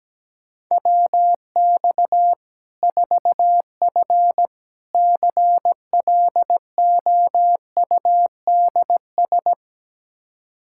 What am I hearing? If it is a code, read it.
Morse code, heard as a keyed tone, about 17 wpm, 700 Hz. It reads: WX 4F CLOUDS
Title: Practice QSO - Exchange 2 - Info - 17wpm